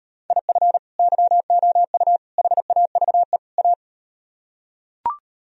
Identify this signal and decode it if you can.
Morse code, heard as a keyed tone, about 38 wpm, 700 Hz. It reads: IF YOU HAVE A